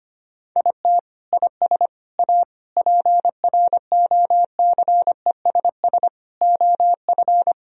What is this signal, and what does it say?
Morse code, heard as a keyed tone, about 25 wpm, 700 Hz. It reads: IT IS A PROCESS OF
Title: Sentences from Top 1000 Words - Review Part 4 - 25wpm